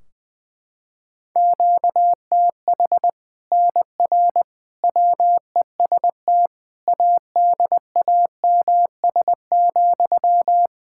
Morse code, heard as a keyed tone, about 21 wpm, 700 Hz. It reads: QTH NR WEST ADAMS,
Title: Practice QSO - Exchange 1 - Beginning - 20wpm